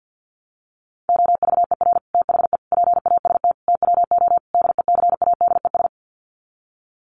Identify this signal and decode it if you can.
Morse code, heard as a keyed tone, about 50 wpm, 700 Hz. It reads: OVER THE PAST TWO DECADES
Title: Sets of 5 Words Encore2 50wpm